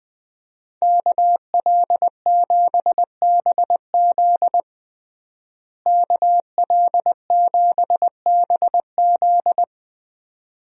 Morse code, heard as a keyed tone, about 20 wpm, 700 Hz. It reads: KL7BZ KL7BZ
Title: DXCC-100 Call Signs - 5 Characters - Repeated 2x - 20wpm